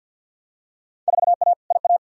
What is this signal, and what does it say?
Morse code, heard as a keyed tone, about 51 wpm, 700 Hz. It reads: 4A IA
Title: Field Day Exchanges - 50wpm